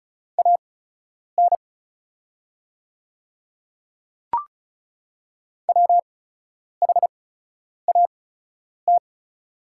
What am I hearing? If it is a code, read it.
Morse code, heard as a keyed tone, about 35 wpm, spaced as for 10 wpm, 690 Hz. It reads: AN WHAT
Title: 8x Character Spacing - Top 200 Words - 35wpm